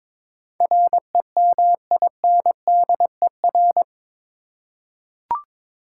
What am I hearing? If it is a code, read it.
Morse code, heard as a keyed tone, about 21 wpm, 700 Hz. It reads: REMINDER
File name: practice / 22wpm / Eight Letter Words - 22wpm